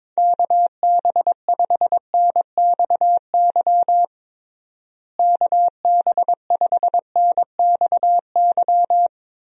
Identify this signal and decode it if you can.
Morse code, heard as a keyed tone, about 22 wpm, 700 Hz. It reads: KB5NXY KB5NXY
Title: USA Call Signs - 6 Characters - Repeated 3x - 22wpm